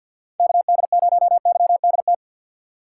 Morse code, heard as a keyed tone, about 52 wpm, 690 Hz. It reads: KD0YDT
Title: USA Call Signs - 6 Characters - 3x Character Spacing - 50wpm